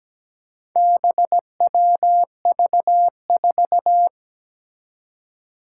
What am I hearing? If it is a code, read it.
Morse code, heard as a keyed tone, about 17 wpm, 700 Hz. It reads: BWV4
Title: Four Challenging Letter-Number Combinations 17wpm